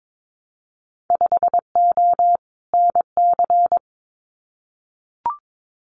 Morse code, heard as a keyed tone, about 22 wpm, 700 Hz. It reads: 5O NC
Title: Winter Field Day Exchanges 22wpm